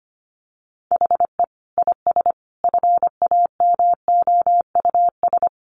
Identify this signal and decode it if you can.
Morse code, heard as a keyed tone, about 25 wpm, 700 Hz. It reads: HE IS FAMOUS
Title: Sentences from Top 1500 Words 25wpm